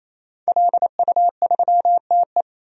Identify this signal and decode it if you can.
Morse code, heard as a keyed tone, about 28 wpm, 700 Hz. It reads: LU3TE
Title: DXCC-100 Call Signs - 5 Characters - 28wpm